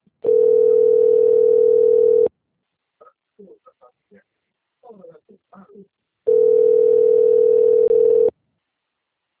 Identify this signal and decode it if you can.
Ringback tone (North America), 440 + 480 Hz, with speech.
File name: rg-1600-unknown- - - .95460